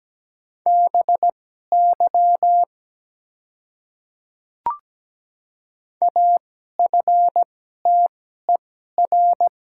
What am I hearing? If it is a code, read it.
Morse code, heard as a keyed tone, about 17 wpm, spaced as for 12 wpm, 700 Hz. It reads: BY AFTER
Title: 2x Character Spacing - Top 200 Words - 17wpm